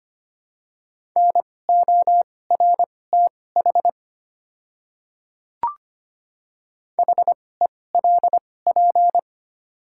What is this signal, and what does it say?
Morse code, heard as a keyed tone, about 25 wpm, spaced as for 18 wpm, 700 Hz. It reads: NORTH HELP
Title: 2x Character Spacing - Top 300 Words - 25wpm